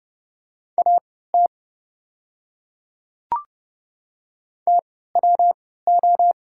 Morse code, heard as a keyed tone, about 30 wpm, spaced as for 17 wpm, 690 Hz. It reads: AT TWO
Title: 3x Character Spacing - Top 200 Words - 30wpm